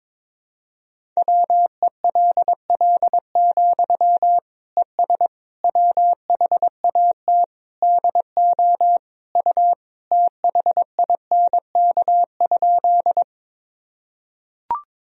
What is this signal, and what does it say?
Morse code, heard as a keyed tone, about 22 wpm, 700 Hz. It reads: WELL, ES WHAT DO U THINK?
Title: Sentences from Top 200 Words and Abbreviations 22wpm